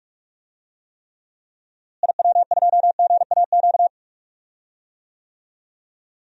Morse code, heard as a keyed tone, about 44 wpm, 700 Hz. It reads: IW2GAQ
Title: DXCC-100 Call Signs - 6 Characters - Repeated 3x - 45wpm